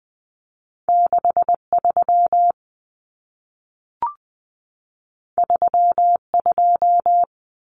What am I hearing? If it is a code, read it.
Morse code, heard as a keyed tone, about 20 wpm, 700 Hz. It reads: 63 32
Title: Common Cut Numbers - 2-Digit 20wpm